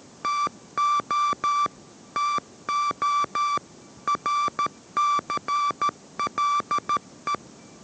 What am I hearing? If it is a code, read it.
Morse code, heard as a keyed tone, about 13 wpm, 1170 Hz. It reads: TO TO RCLE